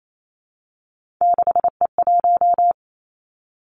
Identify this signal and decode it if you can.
Morse code, heard as a keyed tone, about 28 wpm, 700 Hz. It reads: 6E1
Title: Three Challenging Letter-Number Combinations 28wpm